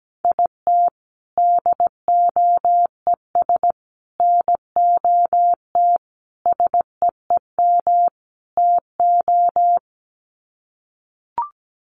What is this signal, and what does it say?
Morse code, heard as a keyed tone, about 17 wpm, 700 Hz. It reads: IT DOES NOT SEEM TO
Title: Sets of 5 Words Encore2 17wpm